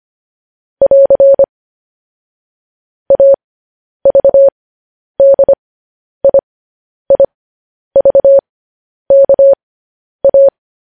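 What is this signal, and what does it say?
Morse code, heard as a keyed tone, about 25 wpm, spaced as for 10 wpm, 550 Hz. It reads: <AR> AVDIIVKA
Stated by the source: Morse code news headlines practice transmission.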